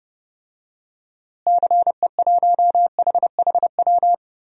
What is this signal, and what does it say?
Morse code, heard as a keyed tone, about 31 wpm, 700 Hz. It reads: CE1HHW
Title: DXCC-100 Call Signs - 6 Characters - Repeated 3x - 30wpm